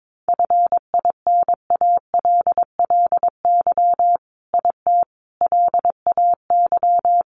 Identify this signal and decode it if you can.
Morse code, heard as a keyed tone, about 22 wpm, 700 Hz. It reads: FINALLY IT LAY
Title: Sentences from Top 1000 Words 22wpm